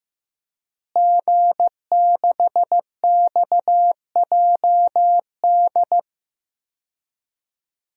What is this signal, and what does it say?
Morse code, heard as a keyed tone, about 15 wpm, 700 Hz. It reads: G6XJD
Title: DXCC-100 Call Signs - 5 Characters - Repeated 2x - 15wpm